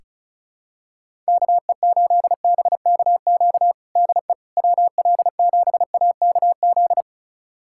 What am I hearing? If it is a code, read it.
Morse code, heard as a keyed tone, about 36 wpm, 700 Hz. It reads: KE8BKQ DE WL7AKZ
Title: USA Callsign DE Callsign - 6 Characters - 35wpm